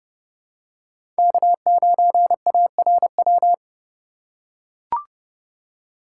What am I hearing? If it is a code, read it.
Morse code, heard as a keyed tone, about 30 wpm, 700 Hz. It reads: K9ARW
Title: USA Call Signs - 5 Characters - 2x Character Spacing - 30wpm